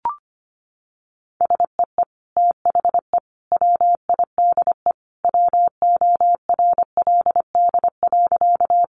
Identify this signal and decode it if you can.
Morse code, heard as a keyed tone, about 25 wpm, 700 Hz. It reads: SEE THE WIDE WORLD.
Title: Sentences from Top 900 Words 25wpm